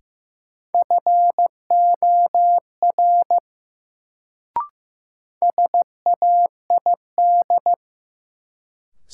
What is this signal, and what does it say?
Morse code, heard as a keyed tone, about 15 wpm, 700 Hz. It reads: FOR SAID
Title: Sentences from Top 100 Words - New Word Drill - 15wpm